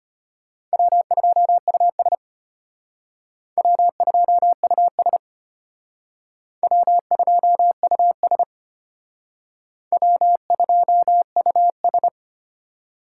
Morse code, heard as a keyed tone, about 31 wpm, 700 Hz. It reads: W2US W2US W2US W2US
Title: Speed Racing - USA 1x2 Call Signs 25wpm